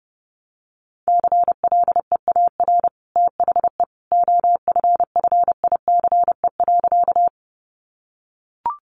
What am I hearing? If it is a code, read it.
Morse code, heard as a keyed tone, about 30 wpm, 700 Hz. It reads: CLEAR THE OFFICE.
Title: Sentences from Top 1000 Words 30wpm